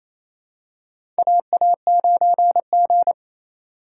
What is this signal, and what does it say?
Morse code, heard as a keyed tone, about 27 wpm, 700 Hz. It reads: AA9G
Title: USA Call Signs - 2x Character Spacing - 28wpm